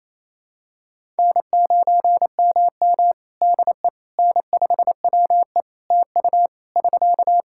Morse code, heard as a keyed tone, about 28 wpm, 700 Hz. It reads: N9MM DE N5WE TU <SK>
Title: Practice QSO - Exchange 4 - Ending - 28wpm